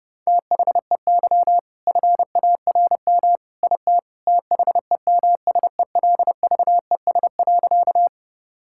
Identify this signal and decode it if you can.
Morse code, heard as a keyed tone, about 30 wpm, 700 Hz. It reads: THEY FARM IT THEMSELVES.
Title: Sentences from Top 1000 Words - Review Part 5 - 30wpm